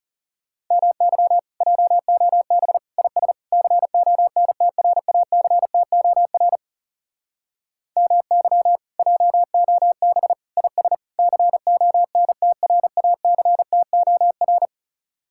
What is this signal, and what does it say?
Morse code, heard as a keyed tone, about 37 wpm, 700 Hz. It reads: MY JOB IS CONTRACTOR MY JOB IS CONTRACTOR
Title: Speed-Racing Common QSO Phrases - 30wpm